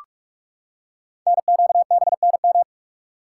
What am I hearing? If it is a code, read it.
Morse code, heard as a keyed tone, about 44 wpm, 700 Hz. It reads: NQBNM
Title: Code Groups - 5 Letters - 45wpm